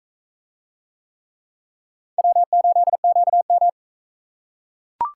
Morse code, heard as a keyed tone, about 42 wpm, 700 Hz. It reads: W8QM